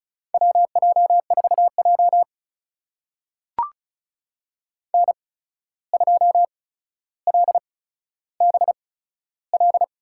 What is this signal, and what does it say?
Morse code, heard as a keyed tone, about 36 wpm, spaced as for 14 wpm, 700 Hz. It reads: WJ4J N2LBL
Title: USA Call Signs - 8x Character Spacing - 35wpm